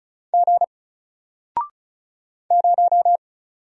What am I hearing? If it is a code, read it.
Morse code, heard as a keyed tone, about 36 wpm, 700 Hz. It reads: G 0
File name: Single-Letter-Number-35wpm